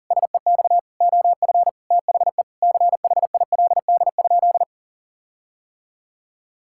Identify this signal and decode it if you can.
Morse code, heard as a keyed tone, about 40 wpm, 700 Hz. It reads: SEX OF THE CHILD?
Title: Sentences from Top 1600 Words 40wpm